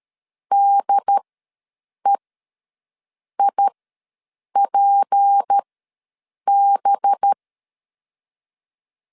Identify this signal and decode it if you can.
Morse code, heard as a keyed tone, about 13 wpm, spaced as for 7 wpm, 800 Hz. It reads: DEIPB
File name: LEZIONI / 6 PKX0 / 6 P ws 12wpm 3117